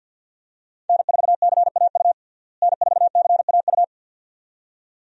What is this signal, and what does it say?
Morse code, heard as a keyed tone, about 52 wpm, 700 Hz. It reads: N4CAU N4CAU
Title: USA Call Signs - 5 Characters - Repeated 2x - 50wpm